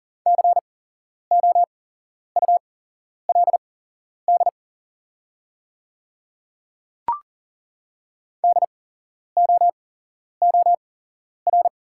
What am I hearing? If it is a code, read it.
Morse code, heard as a keyed tone, about 40 wpm, spaced as for 11 wpm, 700 Hz. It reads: COULD DOOR
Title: 8x Character Spacing - Top 300 Words - 40wpm